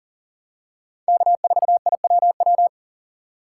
Morse code, heard as a keyed tone, about 38 wpm, 700 Hz. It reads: K4IWW
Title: USA Call Signs - 5 Characters - 4x Character Spacing - 40wpm